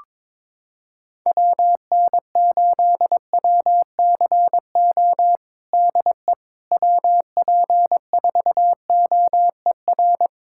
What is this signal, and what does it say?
Morse code, heard as a keyed tone, about 22 wpm, 700 Hz. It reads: WN8WCO DE WP4OER
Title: USA Callsign DE Callsign - 6 Characters - 22wpm